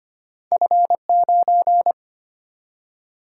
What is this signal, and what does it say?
Morse code, heard as a keyed tone, about 25 wpm, 700 Hz. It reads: F9
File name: Two Letter-Number Combinations - 25wpm